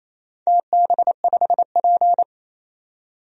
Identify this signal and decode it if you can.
Morse code, heard as a keyed tone, about 28 wpm, 700 Hz. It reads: TB5P
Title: DXCC-100 Call Signs - 4 Characters - 28wpm